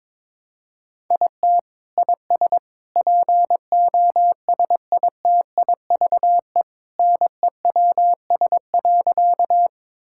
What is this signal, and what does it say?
Morse code, heard as a keyed tone, about 22 wpm, 700 Hz. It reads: IT IS POSITIVE NEWS.
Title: Sentences from Top 1000 Words - Review Part 5 - 22wpm